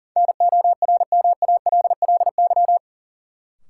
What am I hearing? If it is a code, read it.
Morse code, heard as a keyed tone, about 38 wpm, 700 Hz. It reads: NORMALLY
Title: Sentences from Top 1000 Words - Review Part 6 - New Word Drill - 40wpm